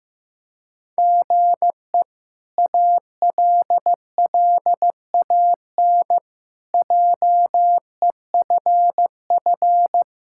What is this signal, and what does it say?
Morse code, heard as a keyed tone, about 15 wpm, 700 Hz. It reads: GE ALLAN JEFF
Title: K1USN SST QSO Practice - 15wpm